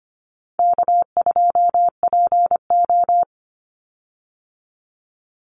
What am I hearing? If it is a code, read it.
Morse code, heard as a keyed tone, about 25 wpm, 700 Hz. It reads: K2PO
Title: Call Signs Repeated 2x - 25wpm